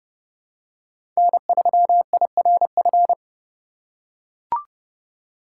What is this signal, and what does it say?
Morse code, heard as a keyed tone, about 30 wpm, 700 Hz. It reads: N3IRF
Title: USA Call Signs - 4x Character Spacing - 30wpm